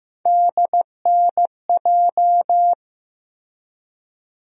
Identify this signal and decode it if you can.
Morse code, heard as a keyed tone, about 15 wpm, 700 Hz. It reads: DNJ
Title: USA Call Sign Postfixes - 3 Letters 15wpm